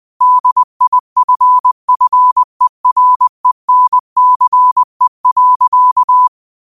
Morse code, heard as a keyed tone, about 20 wpm, 1000 Hz. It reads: DIFFERENCE.
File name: qotdfiles / 20wpm